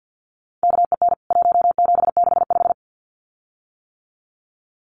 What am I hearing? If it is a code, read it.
Morse code, heard as a keyed tone, about 49 wpm, 700 Hz. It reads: KEN 1765
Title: CWT Exchange 50wpm